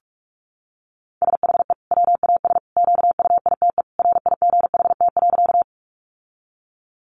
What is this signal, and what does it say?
Morse code, heard as a keyed tone, about 44 wpm, 700 Hz. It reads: SHE WAS QUITE RIGHT.
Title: Sentences from Top 700 Words 45wpm